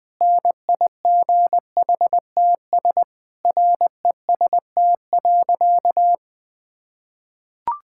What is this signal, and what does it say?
Morse code, heard as a keyed tone, about 21 wpm, 700 Hz. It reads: NIGHTS REST.
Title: Difficult Sentences with Hints - 20wpm